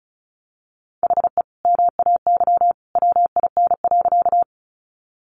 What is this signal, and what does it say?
Morse code, heard as a keyed tone, about 34 wpm, 700 Hz. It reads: HE MAY WIN.
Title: Sentences from Top 1500 Words 35wpm